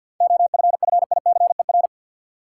Morse code, heard as a keyed tone, about 45 wpm, 700 Hz. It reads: OFFICER
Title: Seven Letter Words - 50wpm